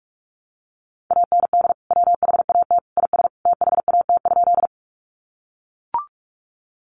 Morse code, heard as a keyed tone, about 44 wpm, 700 Hz. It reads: AND WHAT IS THAT?
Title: Sentences from Top 100 Words 45wpm